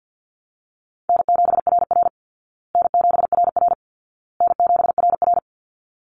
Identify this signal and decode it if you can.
Morse code, heard as a keyed tone, about 50 wpm, 700 Hz. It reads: N7RR N7RR N7RR
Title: Call Signs Repeated 2x - 50wpm